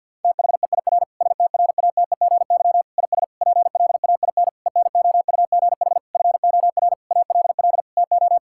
Morse code, heard as a keyed tone, about 52 wpm, 700 Hz. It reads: THEIR STRATEGY IS PLAIN ENOUGH FOR ALL TO
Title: Sentences from Top 1000 Words - Review Part 5 - 50wpm